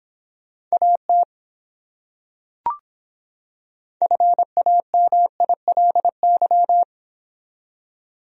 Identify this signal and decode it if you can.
Morse code, heard as a keyed tone, about 26 wpm, 700 Hz. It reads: AT FAMILY